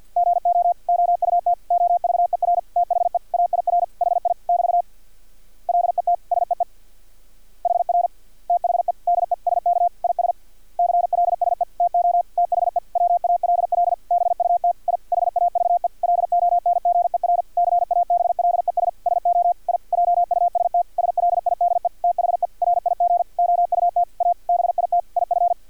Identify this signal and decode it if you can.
Morse code, heard as a keyed tone, about 50 wpm, 700 Hz. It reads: GO OUT OVER THE AIR HI <BT> LET SEE HR THE DESK IS CLSE TO THE WALL BUT I HAVE LONGER CABLES SO I JUST SLIDE THE RIG OUT A BIT IF
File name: archived / CRSnet